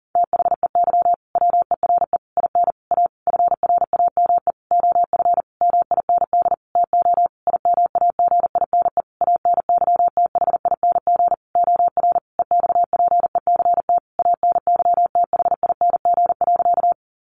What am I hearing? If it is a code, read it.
Morse code, heard as a keyed tone, about 40 wpm, 700 Hz. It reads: THEY WERE IN A FRAME OF MIND TO IMAGINE ANYTHING OR EXPECT ANYTHING.
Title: Sentences from Top 1400 Words 40wpm